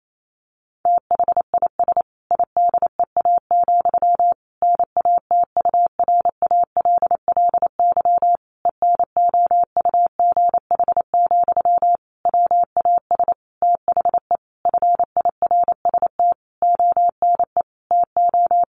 Morse code, heard as a keyed tone, about 28 wpm, 700 Hz. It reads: THIS IDEA, NATURALLY ENOUGH, WAS THE FIRST ONE TO
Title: Sentences from Top 1300 Words 28wpm